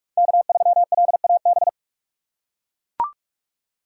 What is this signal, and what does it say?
Morse code, heard as a keyed tone, about 43 wpm, 700 Hz. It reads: K3LAB
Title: USA Call Signs - 5 Characters - 6x Character Spacing - 45wpm